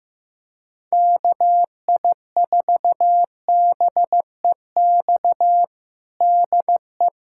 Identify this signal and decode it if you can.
Morse code, heard as a keyed tone, about 16 wpm, 700 Hz. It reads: KI4BEX DE
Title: Practice QSO - Exchange 4 - Ending - 15wpm